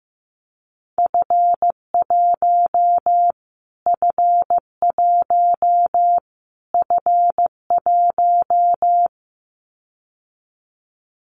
Morse code, heard as a keyed tone, about 15 wpm, 700 Hz. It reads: F1 F1 F1
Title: Callsign Prefixes for Top 25 DXCC - Repeated 3x - 15wpm